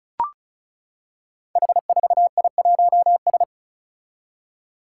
Morse code, heard as a keyed tone, about 34 wpm, 700 Hz. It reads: H4I1S
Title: Five Challenging Letter-Number Combinations 35wpm